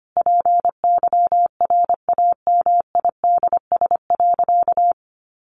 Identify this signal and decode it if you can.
Morse code, heard as a keyed tone, about 25 wpm, 700 Hz. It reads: PYRAMIDS.
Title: A Sound Of Thunder 25wpm